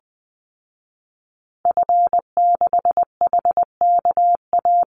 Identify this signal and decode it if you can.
Morse code, heard as a keyed tone, about 20 wpm, 700 Hz. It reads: F6HKA
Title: Call Signs Repeated 2x - 20wpm